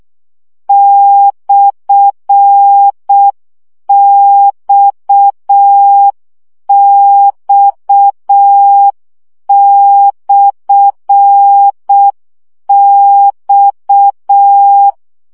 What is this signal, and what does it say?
Morse code, heard as a keyed tone, about 6 wpm, 800 Hz. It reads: /XX/X
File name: CW gyakorlas x per 5 baud